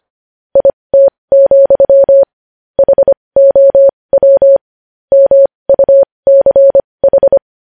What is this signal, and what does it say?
Morse code, heard as a keyed tone, about 25 wpm, spaced as for 20 wpm, 550 Hz. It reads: IT, HOW MUCH
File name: podcast / News headlines-1x-25x20